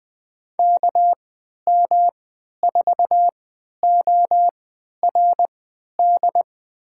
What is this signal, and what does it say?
Morse code, heard as a keyed tone, about 20 wpm, spaced as for 11 wpm, 700 Hz. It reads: KM4ORD
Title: USA Call Signs - 3x Character Spacing - 20wpm